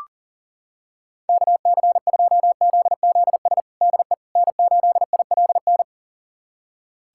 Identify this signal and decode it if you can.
Morse code, heard as a keyed tone, about 40 wpm, 700 Hz. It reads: KC2ZZS DE N8ILN
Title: USA Callsign DE Callsign 40wpm